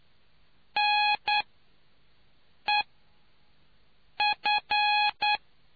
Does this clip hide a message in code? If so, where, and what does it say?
Yes: Morse code from 0.8 to 5.4 s, heard as a keyed tone, about 9 wpm, spaced as for 5 wpm, 840 Hz. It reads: NEF